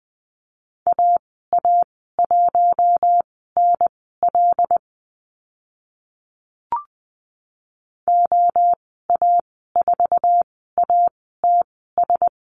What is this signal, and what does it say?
Morse code, heard as a keyed tone, about 20 wpm, spaced as for 14 wpm, 700 Hz. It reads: AA1NL OA4ATS
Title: 076 Lesson 13g - Letter L - Call Signs 20wpm